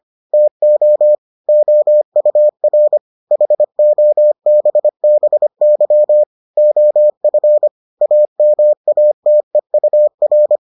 Morse code, heard as a keyed tone, about 25 wpm, 600 Hz. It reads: TO OUR HOBBY OF AMATEUR